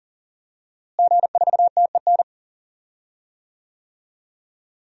Morse code, heard as a keyed tone, about 40 wpm, 700 Hz. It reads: G4TEN
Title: DXCC-100 Call Signs - 5 Characters - Repeated 2x - 40wpm